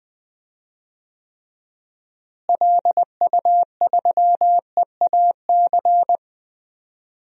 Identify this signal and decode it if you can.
Morse code, heard as a keyed tone, about 20 wpm, 700 Hz. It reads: LU3EAC
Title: DXCC-100 Call Signs - 6 Characters - Repeated 2x - 20wpm